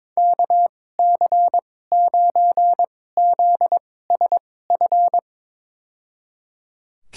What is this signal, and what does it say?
Morse code, heard as a keyed tone, about 22 wpm, spaced as for 16 wpm, 700 Hz. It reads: KC9ZSF
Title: USA Call Signs - 6 Characters - 2x Character Spacing - 22wpm